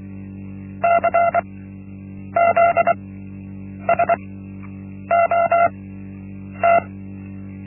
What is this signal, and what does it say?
Morse code, heard as a keyed tone, about 24 wpm, spaced as for 8 wpm, 680 Hz. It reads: CZSOT
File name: Linksmakalnis-1